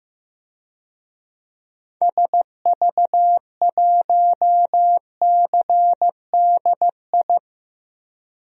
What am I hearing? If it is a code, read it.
Morse code, heard as a keyed tone, about 15 wpm, 700 Hz. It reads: SV1CDI
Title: DXCC-100 Call Signs - 6 Characters - 15wpm